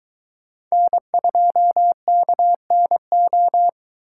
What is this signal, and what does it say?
Morse code, heard as a keyed tone, about 20 wpm, 700 Hz. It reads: N2KNO
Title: Speed Racing - USA 1x3 Call Signs 17wpm